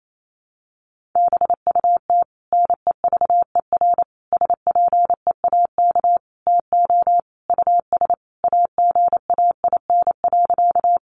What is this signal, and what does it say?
Morse code, heard as a keyed tone, about 28 wpm, 700 Hz. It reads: BUT NEVER SPEAK TO US AGAIN.
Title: Sentences from Top 1100 Words 28wpm